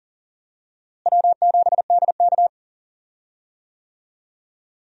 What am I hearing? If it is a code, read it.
Morse code, heard as a keyed tone, about 40 wpm, 700 Hz. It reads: W7DK
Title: USA Call Signs - 4 Characters - Repeated 2x - 40wpm